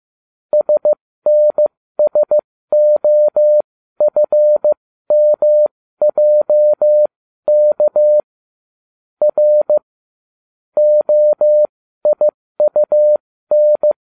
Morse code, heard as a keyed tone, about 15 wpm, spaced as for 13 wpm, 600 Hz. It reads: SNSOFMJK R OIUN